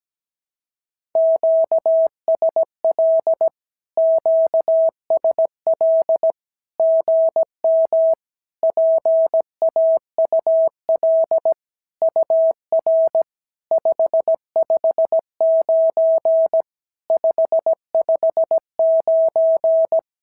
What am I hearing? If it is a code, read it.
Morse code, heard as a keyed tone, about 17 wpm, 650 Hz. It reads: QSL QSL GM PAUL UR 559 559
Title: SOTA QSO Practice 17wpm